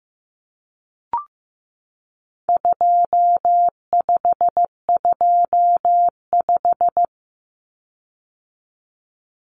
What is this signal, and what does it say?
Morse code, heard as a keyed tone, about 15 wpm, 700 Hz. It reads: 2525
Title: Common Cut Numbers - 4-Digit 15wpm